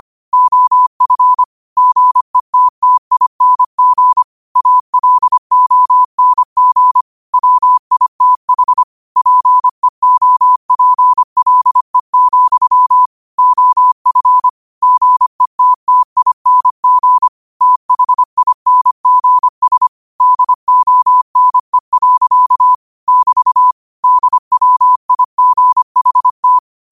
Morse code, heard as a keyed tone, about 25 wpm, 1000 Hz. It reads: OF GETTING ALONG WITH PEOPLE, OF GETTING THINGS DONE. <BT> DWIGHT
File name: qotdfiles / 25wpm